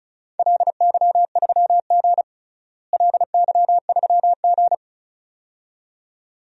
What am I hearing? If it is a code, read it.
Morse code, heard as a keyed tone, about 36 wpm, 690 Hz. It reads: LY3G LY3G
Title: DXCC-100 Call Signs - 4 Characters - Repeated 2x - 35wpm